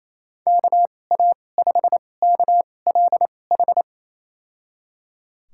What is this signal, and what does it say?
Morse code, heard as a keyed tone, about 28 wpm, spaced as for 20 wpm, 700 Hz. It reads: KA5KLH
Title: USA Call Signs - 6 Characters - 2x Character Spacing - 28wpm